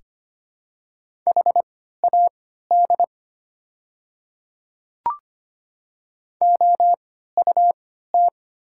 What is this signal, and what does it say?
Morse code, heard as a keyed tone, about 25 wpm, spaced as for 14 wpm, 690 Hz. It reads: HAD OUT
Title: 3x Character Spacing - Top 100 Words - 25wpm